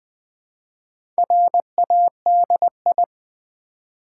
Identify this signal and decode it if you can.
Morse code, heard as a keyed tone, about 20 wpm, 700 Hz. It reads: RADI
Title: Build A Word - 1 Letter At A Time - 3 to 5-Letter Words - 20wpm